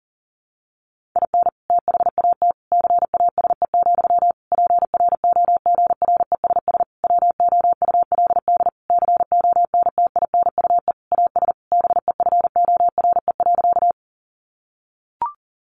Morse code, heard as a keyed tone, about 40 wpm, 700 Hz. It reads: IN THAT CASE, PROGRESS WOULD CONTINUE AS BEFORE.
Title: Sentences from Top 1400 Words 40wpm